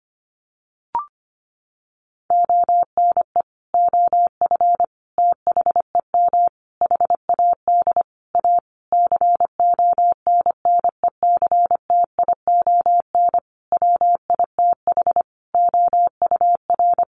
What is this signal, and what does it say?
Morse code, heard as a keyed tone, about 25 wpm, 700 Hz. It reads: ONE OF THEM HAD A CONNECTION WITH OUR
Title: Sentences from Top 1300 Words 25wpm